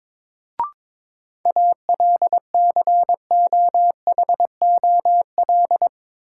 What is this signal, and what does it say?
Morse code, heard as a keyed tone, about 22 wpm, 700 Hz. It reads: ALCOHOL
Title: Sentences from Top 1400 Words - New Word Drill - 22wpm